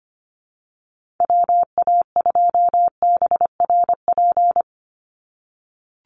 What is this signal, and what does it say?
Morse code, heard as a keyed tone, about 25 wpm, 700 Hz. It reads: WA2BRP
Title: USA Call Signs 25wpm